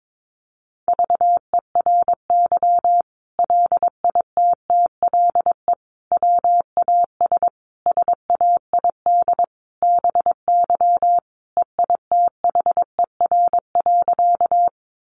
Morse code, heard as a keyed tone, about 22 wpm, 700 Hz. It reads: VERY LITTLE WAS SAID BY EITHER.
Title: Sentences from Top 600 Words 22wpm